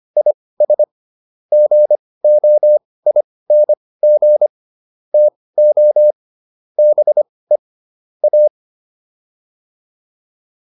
Morse code, heard as a keyed tone, about 25 wpm, spaced as for 18 wpm, 600 Hz. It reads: IS GOING TO BE A